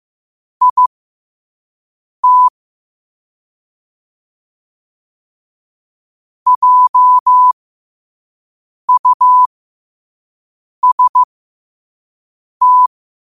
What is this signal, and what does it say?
Morse code, heard as a keyed tone, about 15 wpm, spaced as for 5 wpm, 1000 Hz. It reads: IT JUST
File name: qotdfiles / 5wpm